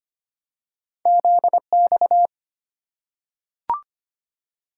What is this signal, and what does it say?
Morse code, heard as a keyed tone, about 25 wpm, 700 Hz. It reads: Z X
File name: Two Letter-Number Combinations - 25wpm